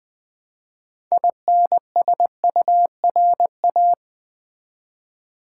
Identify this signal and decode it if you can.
Morse code, heard as a keyed tone, about 20 wpm, 700 Hz. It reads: INSURA
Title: Build A Word - 3 Letters At A Time - 9-Letter Words - 20wpm